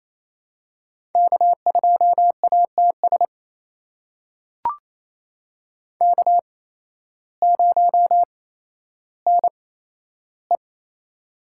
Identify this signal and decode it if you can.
Morse code, heard as a keyed tone, about 28 wpm, spaced as for 13 wpm, 700 Hz. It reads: K2ATS K0NE
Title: USA Call Signs - 8x Character Spacing - 28wpm